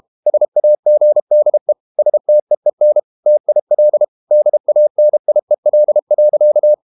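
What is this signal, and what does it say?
Morse code, heard as a keyed tone, about 32 wpm, 600 Hz. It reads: SAGDE STEEN TIL DANIEL.
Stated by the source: High-speed QRQ CW at 32 WPM.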